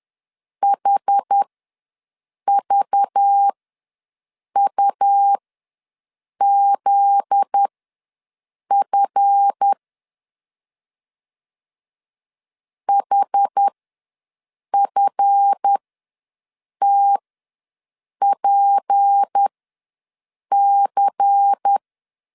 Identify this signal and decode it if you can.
Morse code, heard as a keyed tone, about 10 wpm, spaced as for 6 wpm, 800 Hz. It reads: HVUZF HFTPC
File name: LEZIONI / 7 GF9Z / 7 GF9Z ws 10wpm 3919